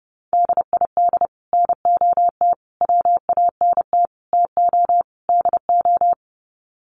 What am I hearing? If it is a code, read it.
Morse code, heard as a keyed tone, about 30 wpm, 700 Hz. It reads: DID NOT WANT TO DO
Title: Sets of 5 Words Encore 30wpm